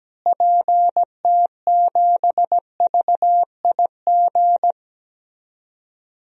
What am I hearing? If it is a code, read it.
Morse code, heard as a keyed tone, about 17 wpm, 700 Hz. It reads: PT7VIG
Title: DXCC-100 Call Signs - 6 Characters - Repeated 3x - 17wpm